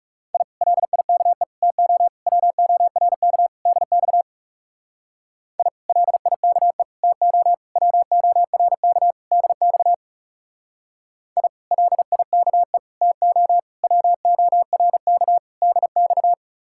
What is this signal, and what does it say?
Morse code, heard as a keyed tone, about 35 wpm, 700 Hz. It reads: I LIKE TO WORK DX I LIKE TO WORK DX I LIKE TO WORK DX
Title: Speed-Racing Common QSO Phrases - 30wpm